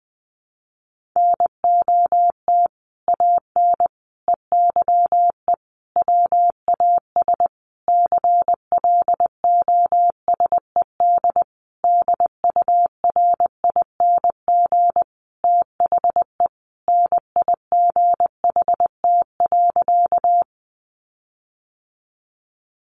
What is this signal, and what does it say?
Morse code, heard as a keyed tone, about 20 wpm, 700 Hz. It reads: NOT AN EYE WAS CLOSED DURING THE NIGHT.
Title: Sentences from Top 1400 Words 20wpm